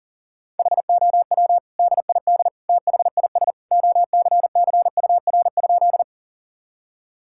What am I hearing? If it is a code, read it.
Morse code, heard as a keyed tone, about 37 wpm, 700 Hz. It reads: HOW DID THIS OCCUR?
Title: Sentences from Top 1000 Words - Review Part 6 - 40wpm